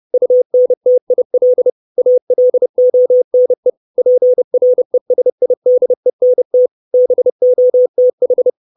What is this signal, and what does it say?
Morse code, heard as a keyed tone, about 30 wpm, 500 Hz. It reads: UNTIL ALONE PRESIDENT BOTH
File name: 632 Words 30WPM